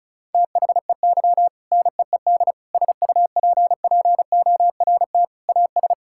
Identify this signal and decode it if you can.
Morse code, heard as a keyed tone, about 34 wpm, 700 Hz. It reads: THEY NEED SUPPORT AS
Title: Sentences from Top 1000 Words - Review Part 5 - 35wpm